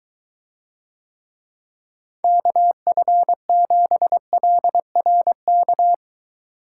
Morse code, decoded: KF7LRK

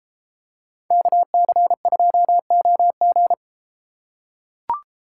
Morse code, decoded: KC2OG

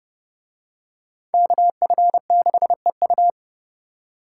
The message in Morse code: KF6EU

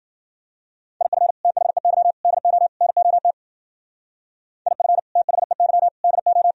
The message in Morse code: IF THEY DO NOT IF THEY DO

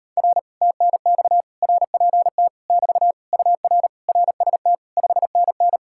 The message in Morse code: R TNX RPT <BT> UR RST 5NN

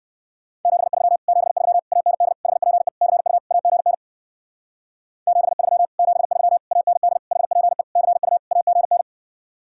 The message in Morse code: 73 73 AND HPE CU AGN 73 73 AND HPE CU AGN

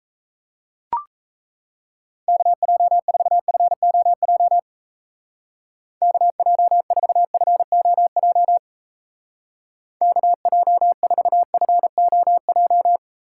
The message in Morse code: KJ4FOJ KJ4FOJ KJ4FOJ

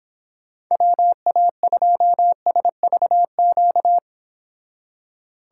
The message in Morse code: WA2SVQ

